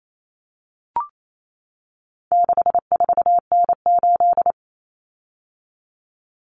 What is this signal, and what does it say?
Morse code, heard as a keyed tone, about 28 wpm, 700 Hz. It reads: 64N8
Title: Common Cut Numbers - 4-Digit 28wpm